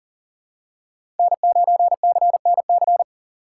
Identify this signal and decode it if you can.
Morse code, heard as a keyed tone, about 40 wpm, 700 Hz. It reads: N9CNC
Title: USA Call Signs - 5 Characters - 3x Character Spacing - 40wpm